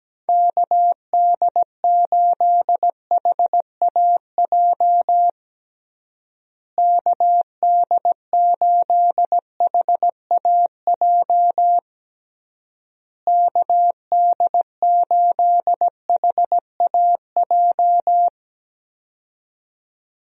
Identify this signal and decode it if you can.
Morse code, heard as a keyed tone, about 17 wpm, 700 Hz. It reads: KD8HAJ KD8HAJ KD8HAJ
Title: USA Call Signs - 6 Characters - Repeated 3x - 17wpm